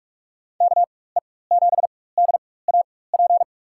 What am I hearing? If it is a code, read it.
Morse code, heard as a keyed tone, about 44 wpm, spaced as for 21 wpm, 700 Hz. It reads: KE7DAP
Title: USA Call Signs - 4x Character Spacing - 45wpm